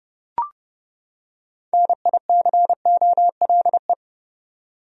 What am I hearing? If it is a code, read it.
Morse code, heard as a keyed tone, about 31 wpm, 700 Hz. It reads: NICOLE
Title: USA Common Names - 30wpm